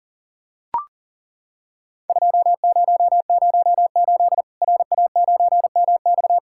Morse code, heard as a keyed tone, about 37 wpm, 700 Hz. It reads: 2008 RA9MX
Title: Worked All Europe DX Contest - QTC Exchange - 40wpm